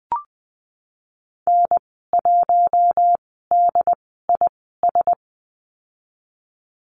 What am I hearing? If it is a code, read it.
Morse code, heard as a keyed tone, about 20 wpm, spaced as for 15 wpm, 700 Hz. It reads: N1DIS
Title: 140 Lesson 23g - Letter f - Call Signs 20wpm